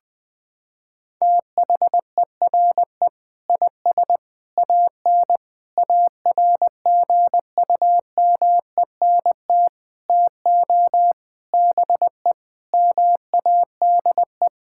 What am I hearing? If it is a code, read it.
Morse code, heard as a keyed tone, about 20 wpm, 700 Hz. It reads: THERE IS AN ARGUMENT TO BE MADE
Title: Sets of 8 Words - 20wpm